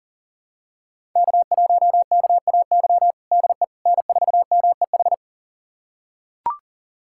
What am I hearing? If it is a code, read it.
Morse code, heard as a keyed tone, about 37 wpm, 700 Hz. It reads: K1KAY DE N4MEH
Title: USA Callsign DE Callsign - 5 Characters - 40wpm